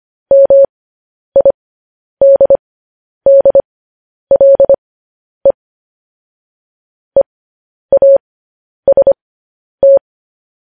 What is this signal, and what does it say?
Morse code, heard as a keyed tone, about 25 wpm, spaced as for 10 wpm, 550 Hz. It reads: MIDDLE EAST